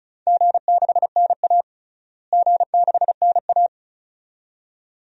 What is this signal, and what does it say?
Morse code, heard as a keyed tone, about 36 wpm, 700 Hz. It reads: G6NA G6NA
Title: DXCC-100 Call Signs - 4 Characters - Repeated 2x - 35wpm